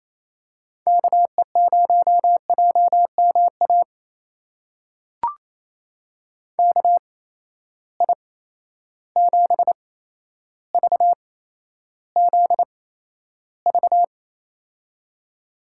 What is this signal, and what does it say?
Morse code, heard as a keyed tone, about 28 wpm, spaced as for 12 wpm, 700 Hz. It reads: KE0JMA KI7VZV
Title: USA Call Signs - 6 Characters - 8x Character Spacing - 28wpm